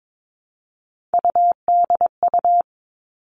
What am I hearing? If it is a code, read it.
Morse code, heard as a keyed tone, about 22 wpm, 700 Hz. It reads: UDU
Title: Three Challenging Letter-Number Combinations 22wpm